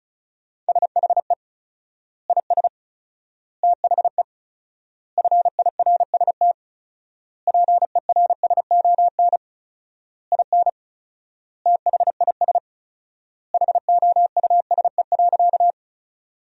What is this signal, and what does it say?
Morse code, heard as a keyed tone, about 34 wpm, 700 Hz. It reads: SHE IS THE FIRST PERSON IN THIS HOUSE.
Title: 4x Word-Spacing - Sentences from Top 400 Words - 35wpm